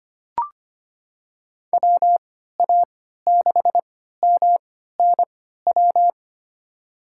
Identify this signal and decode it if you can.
Morse code, heard as a keyed tone, about 25 wpm, spaced as for 14 wpm, 700 Hz. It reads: WA6MNW
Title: USA Call Signs - 6 Characters - 3x Character Spacing - 25wpm